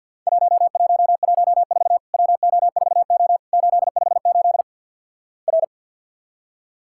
Morse code, heard as a keyed tone, about 50 wpm, 700 Hz. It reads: 1114 WO4O 858 R
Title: Worked All Europe DX Contest - QTC Exchange with Repeats - 50wpm